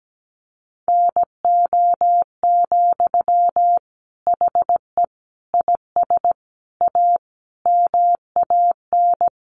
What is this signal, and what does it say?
Morse code, heard as a keyed tone, about 17 wpm, 700 Hz. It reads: NO, HE IS A MAN
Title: Sentences from Top 400 Words 17wpm